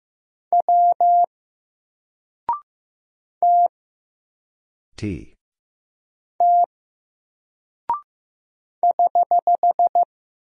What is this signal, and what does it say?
Morse code, heard as a keyed tone, about 15 wpm, 700 Hz. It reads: W T T <HH>
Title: Single Character (Letter:Number:Punctuation:Prosign) 15wpm